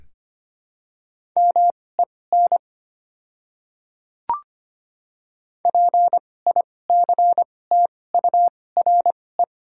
Morse code, heard as a keyed tone, about 25 wpm, spaced as for 18 wpm, 700 Hz. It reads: MEN PICTURE
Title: 2x Character Spacing - Top 200 Words - 25wpm